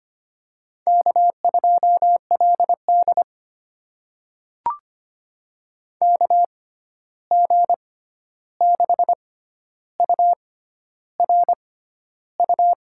Morse code, heard as a keyed tone, about 25 wpm, spaced as for 11 wpm, 700 Hz. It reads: K2LD KG6URU
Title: USA Call Signs - 6x Character Spacing - 25wpm